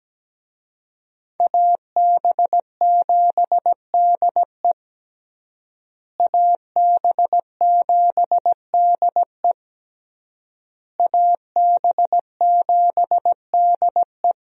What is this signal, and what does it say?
Morse code, heard as a keyed tone, about 17 wpm, 700 Hz. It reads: AB7DE AB7DE AB7DE
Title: USA Call Signs - 5 Characters - Repeated 3x - 17wpm